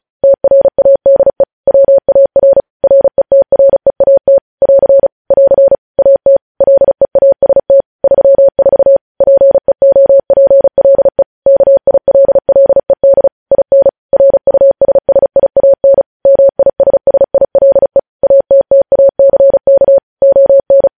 Morse code, heard as a keyed tone, about 36 wpm, 550 Hz. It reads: TRADE WAR RETREAT <AR> <AR> AT LEAST 34 PEOPLE KILLED IN RUSSIAN MISSILE ATTACK ON
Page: news headlines in morse code at 35 wpm